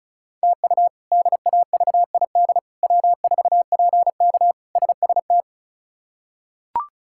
Morse code, heard as a keyed tone, about 34 wpm, 700 Hz. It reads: TU DAVID W4PK SST
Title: K1USN SST QSO Practice - 35wpm